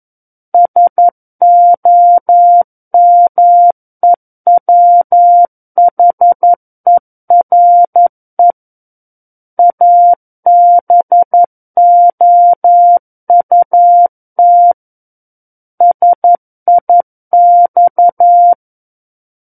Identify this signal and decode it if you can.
Morse code, heard as a keyed tone, about 11 wpm, 700 Hz. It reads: SOMEWHERE ABOUT SIX
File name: Book One / 02-Chapter 2 - 11 WPM